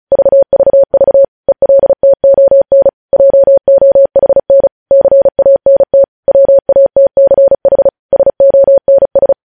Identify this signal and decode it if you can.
Morse code, heard as a keyed tone, about 36 wpm, 550 Hz. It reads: VVV ELTON JOHN CANT WATCH SONS
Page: news headlines in morse code at 35 wpm